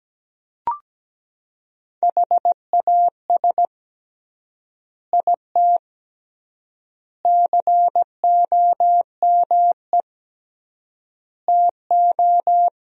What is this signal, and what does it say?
Morse code, heard as a keyed tone, about 17 wpm, 700 Hz. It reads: HAS IT COME TO
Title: 3x Word-Spacing - Sentences from Top 100 Words - 17wpm